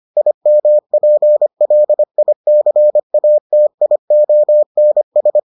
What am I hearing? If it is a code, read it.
Morse code, heard as a keyed tone, about 25 wpm, 600 Hz. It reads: IMPLICATIONS